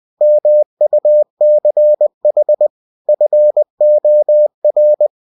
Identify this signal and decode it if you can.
Morse code, heard as a keyed tone, about 20 wpm, 600 Hz. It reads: MUCH FOR
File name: cw-books / winnie-the-pooh / winnie-20wpm-page-0085